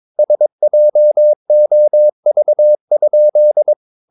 Morse code, heard as a keyed tone, about 22 wpm, 600 Hz. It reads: SJOV?
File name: Ferie QRS 22wpm